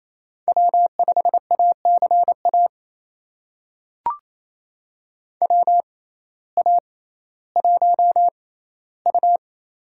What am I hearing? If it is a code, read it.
Morse code, heard as a keyed tone, about 27 wpm, spaced as for 15 wpm, 700 Hz. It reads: W5ACA WA1U
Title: USA Call Signs - 6x Character Spacing - 28wpm